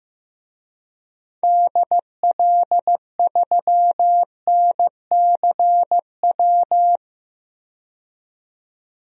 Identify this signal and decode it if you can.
Morse code, heard as a keyed tone, about 15 wpm, 700 Hz. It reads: DL3NCW
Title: DXCC-100 Call Signs - 6 Characters - 15wpm